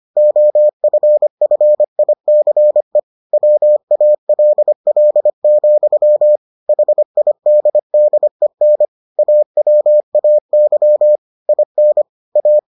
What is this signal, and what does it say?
Morse code, heard as a keyed tone, about 25 wpm, 600 Hz. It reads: OFFICE WALL, HIDDEN AWAY IN A